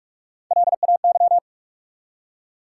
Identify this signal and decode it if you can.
Morse code, heard as a keyed tone, about 47 wpm, 700 Hz. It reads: LAY
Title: Three Letter Words - 45wpm